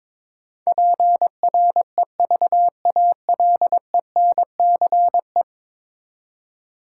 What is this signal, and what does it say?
Morse code, heard as a keyed tone, about 22 wpm, 700 Hz. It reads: PREVALENCE
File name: practice / 22wpm / 10 Letter Words - First 3 Letters Given - 22wpm